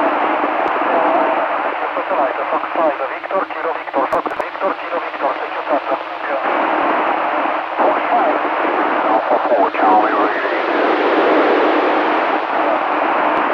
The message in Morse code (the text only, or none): none